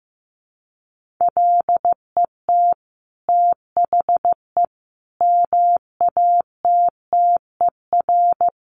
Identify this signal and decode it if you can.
Morse code, heard as a keyed tone, about 15 wpm, 700 Hz. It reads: LET THE MATTER
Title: Sentences from Top 1400 Words 15wpm